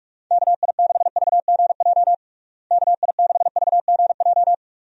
Morse code, heard as a keyed tone, about 46 wpm, 700 Hz. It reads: KI6VGJ KI6VGJ
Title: USA Call Signs - 6 Characters - Repeated 2x - 45wpm